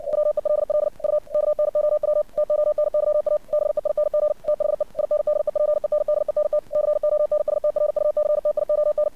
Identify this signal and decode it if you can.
Morse code, heard as a keyed tone, about 59 wpm, 600 Hz. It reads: WELL U KNOW TONYA BEING THE INDEPENDENT CONSTRUCTION
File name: archived / CRSnet